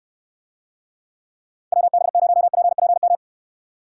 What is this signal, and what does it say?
Morse code, heard as a keyed tone, about 69 wpm, 700 Hz. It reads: WB0PPM